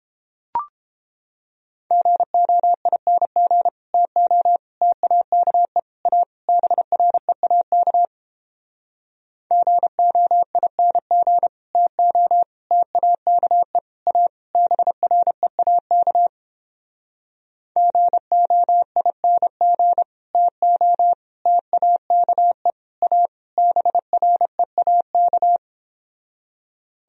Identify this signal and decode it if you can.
Morse code, heard as a keyed tone, about 31 wpm, 700 Hz. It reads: GOING TO TAKE A BREAK GOING TO TAKE A BREAK GOING TO TAKE A BREAK